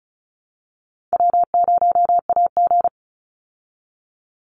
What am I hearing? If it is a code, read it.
Morse code, heard as a keyed tone, about 34 wpm, 700 Hz. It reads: W0AG
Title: Call Signs Repeated 2x - 35wpm